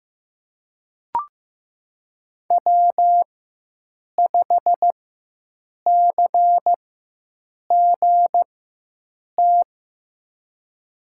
Morse code, heard as a keyed tone, about 15 wpm, spaced as for 7 wpm, 700 Hz. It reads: W5CGT
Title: USA Call Signs - 5 Characters - 4x Character Spacing - 15wpm